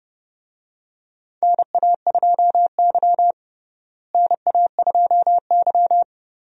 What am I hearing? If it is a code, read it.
Morse code, heard as a keyed tone, about 31 wpm, 700 Hz. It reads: NA2Y NA2Y
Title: USA Call Signs - 4 Characters - Repeated 3x - 30wpm